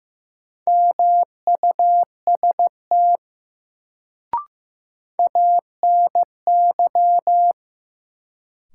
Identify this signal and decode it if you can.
Morse code, heard as a keyed tone, about 15 wpm, 700 Hz. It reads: MUST ANY
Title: Sentences from Top 200 Words - New Word Drill - 15wpm